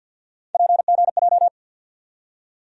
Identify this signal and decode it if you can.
Morse code, heard as a keyed tone, about 49 wpm, 700 Hz. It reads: PGJ
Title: USA Call Sign Postfixes - 3 Letters 50wpm